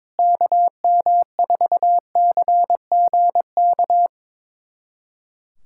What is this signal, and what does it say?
Morse code, heard as a keyed tone, about 22 wpm, 700 Hz. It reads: KM4CGK
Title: USA Callsign DE Callsign - 6 Characters - 22wpm